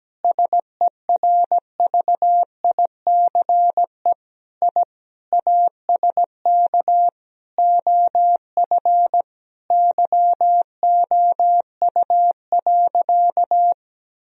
Morse code, heard as a keyed tone, about 17 wpm, 700 Hz. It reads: SERVICE I ASK OF YOU.
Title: Sentences from Top 1000 Words - Review Part 4 - 17wpm